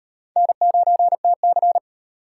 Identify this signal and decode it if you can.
Morse code, heard as a keyed tone, about 38 wpm, 700 Hz. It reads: N9TC